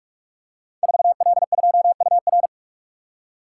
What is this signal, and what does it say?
Morse code, heard as a keyed tone, about 44 wpm, 700 Hz. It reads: 4L2UR
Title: Code Groups - 5 Letter:Numbers - 45wpm